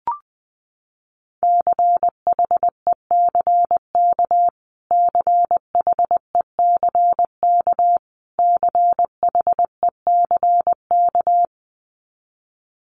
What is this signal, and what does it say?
Morse code, heard as a keyed tone, about 20 wpm, 700 Hz. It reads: CHECK CHECK CHECK
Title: Top 500 Words Repeated 20wpm